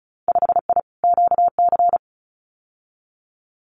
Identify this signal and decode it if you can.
Morse code, heard as a keyed tone, about 34 wpm, 700 Hz. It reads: 5I QC
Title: Winter Field Day Exchanges Encore 35wpm